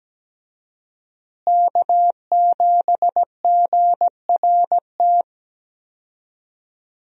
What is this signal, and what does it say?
Morse code, heard as a keyed tone, about 17 wpm, 700 Hz. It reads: K7GRT
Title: USA Call Signs - 5 Characters - Repeated 3x - 17wpm